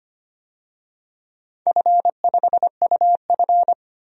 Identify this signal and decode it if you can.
Morse code, heard as a keyed tone, about 25 wpm, 700 Hz. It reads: F5UF
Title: DXCC-100 Call Signs - 4 Characters - 25wpm